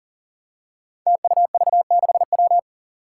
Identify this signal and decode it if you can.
Morse code, heard as a keyed tone, about 40 wpm, 700 Hz. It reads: TUV6W
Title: Five Challenging Letter-Number Combinations 40wpm